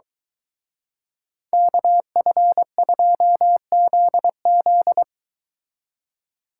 Morse code, heard as a keyed tone, about 23 wpm, 700 Hz. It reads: KF2ZZ